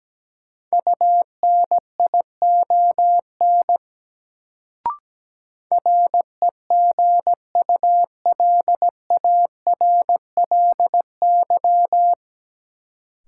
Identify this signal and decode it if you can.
Morse code, heard as a keyed tone, about 17 wpm, 700 Hz. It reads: UNION REGULARLY
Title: Sentences from Top 1400 Words - New Word Drill - 17wpm